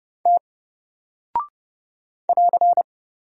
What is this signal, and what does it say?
Morse code, heard as a keyed tone, about 30 wpm, 700 Hz. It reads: T <AR>
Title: Single Character (Letter:Number:Punctuation:Prosign) Alt 30wpm